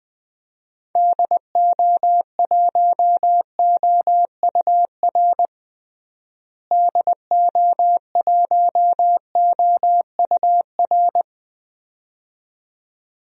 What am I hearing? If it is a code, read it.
Morse code, heard as a keyed tone, about 20 wpm, 700 Hz. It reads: DO1OUR DO1OUR
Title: DXCC-100 Call Signs - 6 Characters - Repeated 2x - 20wpm